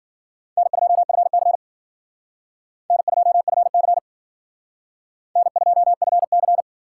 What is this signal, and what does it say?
Morse code, heard as a keyed tone, about 51 wpm, 700 Hz. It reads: N2FC N2FC N2FC